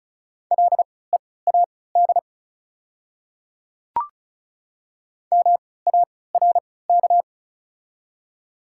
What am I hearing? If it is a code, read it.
Morse code, heard as a keyed tone, about 34 wpm, spaced as for 20 wpm, 690 Hz. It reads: LEAD MARK
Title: 3x Character Spacing - Top 500 Words - 35wpm